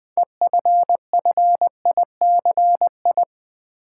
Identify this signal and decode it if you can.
Morse code, heard as a keyed tone, about 20 wpm, 700 Hz. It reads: EFFICI
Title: Build A Word - 3 Letters At A Time - 9-Letter Words - 20wpm